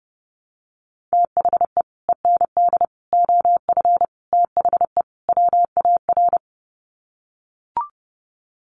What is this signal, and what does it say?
Morse code, heard as a keyed tone, about 30 wpm, 700 Hz. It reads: THE END OF THE WAR
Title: Sets of 5 Words 30wpm